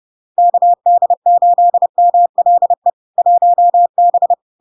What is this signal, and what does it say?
Morse code, heard as a keyed tone, about 30 wpm, 700 Hz. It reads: KD8MLE 1B
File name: LARK-FD-2022-Log-in-CW-30wpm-700-Hz